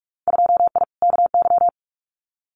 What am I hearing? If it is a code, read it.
Morse code, heard as a keyed tone, about 46 wpm, 700 Hz. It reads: 2I KY
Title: Winter Field Day Exchanges 45wpm